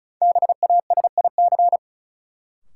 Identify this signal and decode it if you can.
Morse code, heard as a keyed tone, about 36 wpm, 700 Hz. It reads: BASIC
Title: Sentences from Top 1000 Words - Review Part 3 - New Word Drill - 35wpm